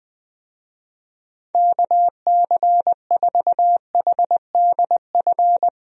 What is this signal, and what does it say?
Morse code, heard as a keyed tone, about 20 wpm, 700 Hz. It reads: KC4HDF
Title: USA Call Signs - 6 Characters - Repeated 3x - 20wpm